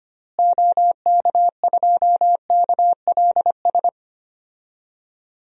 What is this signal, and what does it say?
Morse code, heard as a keyed tone, about 25 wpm, 700 Hz. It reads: OK2KLS
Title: DXCC-100 Call Signs - 6 Characters - Repeated 3x - 25wpm